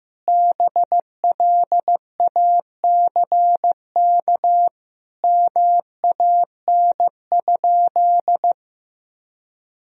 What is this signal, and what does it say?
Morse code, heard as a keyed tone, about 15 wpm, 700 Hz. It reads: BLACK MAN?
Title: Sentences from Top 1000 Words - Review Part 2 - 15wpm